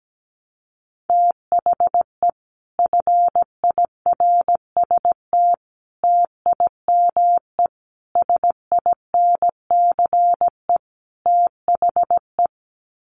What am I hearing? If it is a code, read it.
Morse code, heard as a keyed tone, about 17 wpm, 700 Hz. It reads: THE FIRST TIME SINCE THE
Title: Sets of 5 Words Encore 17wpm